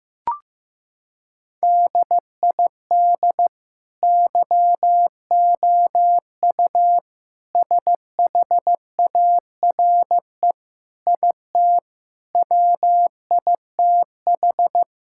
Morse code, heard as a keyed tone, about 15 wpm, 700 Hz. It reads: DID YOU SHARE IT WITH
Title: Sentences from Top 1000 Words - Review Part 6 - 15wpm